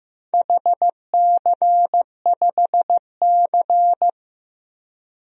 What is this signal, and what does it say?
Morse code, heard as a keyed tone, about 15 wpm, 700 Hz. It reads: HC5C
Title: DXCC-100 Call Signs - 4 Characters - Repeated 3x - 15wpm